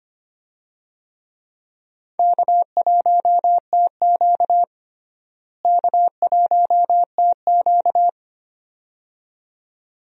Morse code, heard as a keyed tone, about 25 wpm, 700 Hz. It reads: K1TQ K1TQ
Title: USA Call Signs - 4 Characters - Repeated 2x - 25wpm